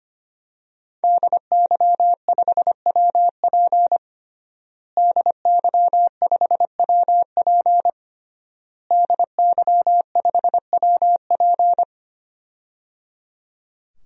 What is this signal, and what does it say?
Morse code, heard as a keyed tone, about 25 wpm, 700 Hz. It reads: DY5WP DY5WP DY5WP
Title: DXCC-100 Call Signs - 5 Characters - Repeated 3x - 25wpm